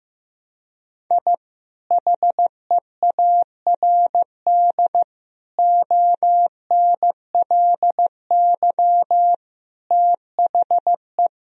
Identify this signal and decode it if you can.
Morse code, heard as a keyed tone, about 15 wpm, 700 Hz. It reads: I HEARD ONLY THE
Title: Sentences from Top 1000 Words - Review Part 2 - 15wpm